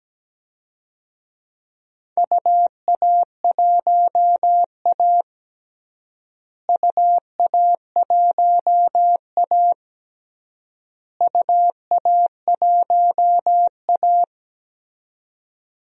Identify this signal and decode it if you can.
Morse code, heard as a keyed tone, about 17 wpm, 700 Hz. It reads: UA1A UA1A UA1A
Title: DXCC-100 Call Signs - 4 Characters - Repeated 3x - 17wpm